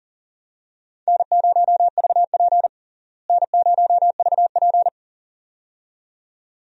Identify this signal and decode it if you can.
Morse code, heard as a keyed tone, about 40 wpm, 700 Hz. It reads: N0VP N0VP
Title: USA Call Signs - 4 Characters - Repeated 2x - 40wpm